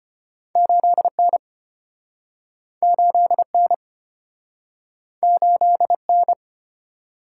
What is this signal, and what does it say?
Morse code, heard as a keyed tone, about 34 wpm, 700 Hz. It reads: 8N 8N 8N